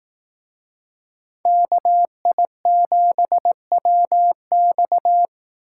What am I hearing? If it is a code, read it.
Morse code, heard as a keyed tone, about 18 wpm, 700 Hz. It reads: KI7WX
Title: Speed Racing - USA 2x2 Call Signs 15wpm